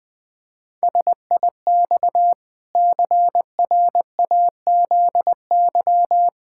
Morse code, heard as a keyed tone, about 20 wpm, 700 Hz. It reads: SIX CRAZY
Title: Pangram Challenge v2 - 20wpm